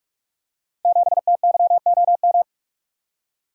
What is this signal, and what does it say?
Morse code, heard as a keyed tone, about 44 wpm, 700 Hz. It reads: 7TYOM
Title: Code Groups - 5 Letter:Numbers - 45wpm